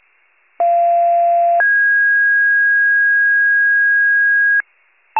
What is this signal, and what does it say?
Two-tone page, heard as a one-second tone, then a three-second tone, about 690 then 1650 Hz. It is heard from 0.6 to 4.6 s.